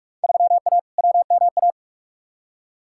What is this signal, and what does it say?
Morse code, heard as a keyed tone, about 46 wpm, 700 Hz. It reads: 3A WMA
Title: Field Day Exchanges - 45wpm